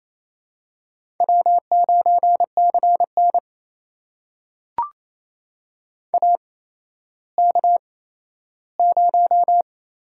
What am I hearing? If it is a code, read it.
Morse code, heard as a keyed tone, about 28 wpm, 700 Hz. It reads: W 9 C N A K 0
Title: USA Call Signs - 4 Characters - 8x Character Spacing - 28wpm